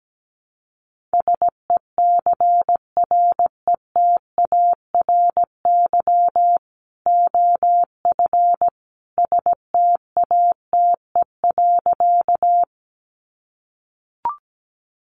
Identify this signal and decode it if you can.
Morse code, heard as a keyed tone, about 17 wpm, 700 Hz. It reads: SECRETARY OF STATE.
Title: Sentences from Top 1300 Words 17wpm